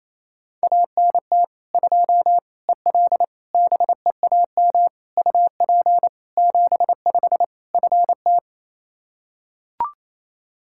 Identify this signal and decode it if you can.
Morse code, heard as a keyed tone, about 28 wpm, 700 Hz. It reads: ANT 2 EL BEAM UP 75 FT
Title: Practice QSO - Exchange 2 - Info - 28wpm